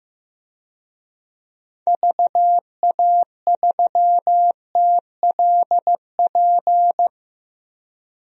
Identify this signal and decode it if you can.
Morse code, heard as a keyed tone, about 15 wpm, 700 Hz. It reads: VA3TLP
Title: DXCC-100 Call Signs - 6 Characters - 15wpm